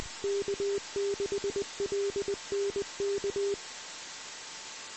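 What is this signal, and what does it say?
Morse code, heard as a keyed tone, about 20 wpm, 400 Hz, with noise. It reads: K6LNK
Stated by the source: Morse code received on the 2 m band, 146.115 MHz.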